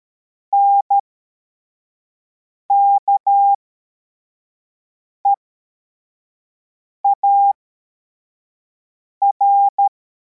Morse code, heard as a keyed tone, about 13 wpm, spaced as for 4 wpm, 800 Hz. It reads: NKEAR